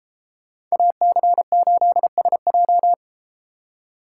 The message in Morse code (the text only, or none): AC8SJ